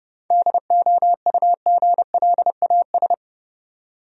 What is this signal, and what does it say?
Morse code, heard as a keyed tone, about 32 wpm, 700 Hz. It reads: DOUGLAS
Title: USA Common Names - 30wpm